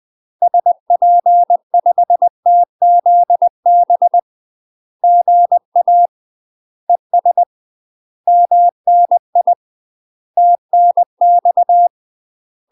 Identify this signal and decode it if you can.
Morse code, heard as a keyed tone, about 20 wpm, 700 Hz. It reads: SP5TZB GA ES MNI TNX
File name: UK MASTER QSOs TRAINING / 20WPM / UK MASTER QSOs 0003